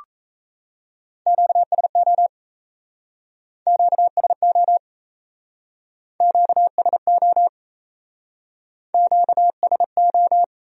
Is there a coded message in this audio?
Yes: Morse code from 1.3 to 10.4 s, heard as a keyed tone, about 37 wpm, 700 Hz. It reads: QSO QSO QSO QSO